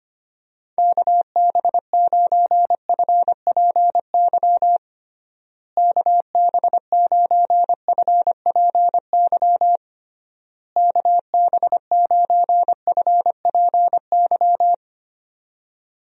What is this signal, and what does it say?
Morse code, heard as a keyed tone, about 25 wpm, 700 Hz. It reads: KB9FPY KB9FPY KB9FPY
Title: USA Call Signs - 6 Characters - Repeated 3x - 25wpm